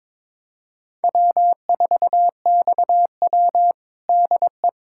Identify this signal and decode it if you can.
Morse code, heard as a keyed tone, about 22 wpm, 700 Hz. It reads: W4XW DE
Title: USA Callsign DE Callsign - 4 Characters - 22wpm